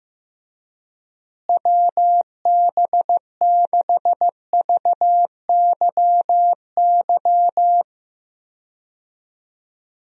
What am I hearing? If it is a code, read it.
Morse code, heard as a keyed tone, about 15 wpm, 700 Hz. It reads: WB6VYY
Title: USA Call Signs - 6 Characters - Repeated 2x - 15wpm